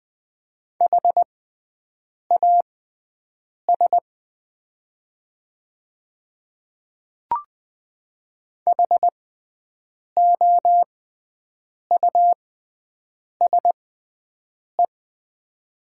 Morse code, heard as a keyed tone, about 20 wpm, spaced as for 7 wpm, 700 Hz. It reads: HAS HOUSE
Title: 6x Character Spacing - Top 200 Words - 20wpm